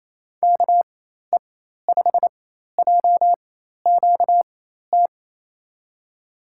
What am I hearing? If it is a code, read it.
Morse code, heard as a keyed tone, about 28 wpm, spaced as for 13 wpm, 700 Hz. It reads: KE5JQT
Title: USA Call Signs - 6 Characters - 4x Character Spacing - 28wpm